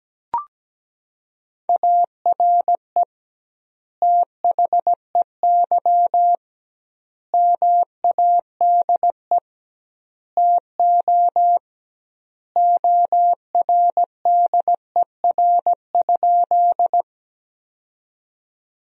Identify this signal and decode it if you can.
Morse code, heard as a keyed tone, about 17 wpm, 700 Hz. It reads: ARE THEY MADE TO ORDER?
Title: 2x Word-Spacing - Sentences from Top 400 Words - 17wpm